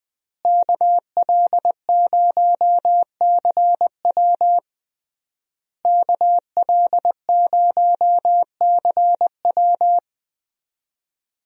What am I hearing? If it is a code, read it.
Morse code, heard as a keyed tone, about 20 wpm, 700 Hz. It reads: KL0CW KL0CW
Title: USA Call Signs - 5 Characters - Repeated 3x - 20wpm